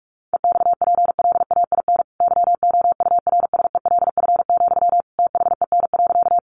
Morse code, heard as a keyed tone, about 44 wpm, 700 Hz. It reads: EXPLAIN YOURSELF, THEN.
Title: Sentences from Top 1100 Words 45wpm